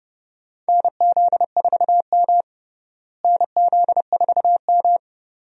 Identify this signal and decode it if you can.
Morse code, heard as a keyed tone, about 30 wpm, 700 Hz. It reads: NZ4M NZ4M
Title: USA Call Signs - 4 Characters - Repeated 2x - 30wpm